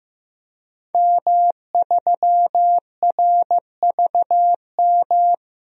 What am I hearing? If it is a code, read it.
Morse code, heard as a keyed tone, about 15 wpm, 700 Hz. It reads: M3RVM
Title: DXCC-100 Call Signs - 5 Characters - 15wpm